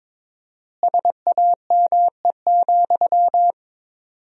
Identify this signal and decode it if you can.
Morse code, heard as a keyed tone, about 22 wpm, 700 Hz. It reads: SAME,
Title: 4x Word-Spacing - Sentences from Top 300 Words - 22wpm